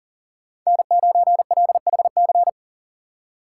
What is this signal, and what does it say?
Morse code, heard as a keyed tone, about 40 wpm, 700 Hz. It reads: N9LHC
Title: USA Call Signs - 5 Characters - 2x Character Spacing - 40wpm